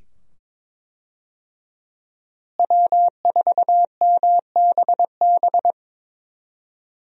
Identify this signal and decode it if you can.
Morse code, heard as a keyed tone, about 22 wpm, 700 Hz. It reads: W4MBB
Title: USA Call Signs - 5 Characters - Repeated 2x - 22wpm